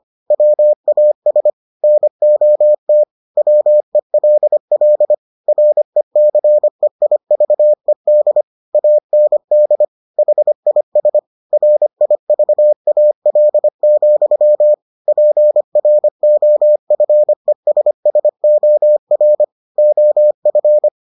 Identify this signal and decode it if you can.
Morse code, heard as a keyed tone, about 25 wpm, 600 Hz. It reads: WAS NOT WELL RECEIVED AND HIS RIVAL, PROFESSOR OF